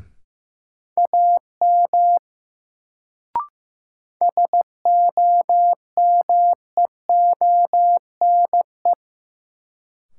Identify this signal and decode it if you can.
Morse code, heard as a keyed tone, about 15 wpm, 700 Hz. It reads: AM SOMEONE
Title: Sentences from Top 600 Words - New Word Drill - 15wpm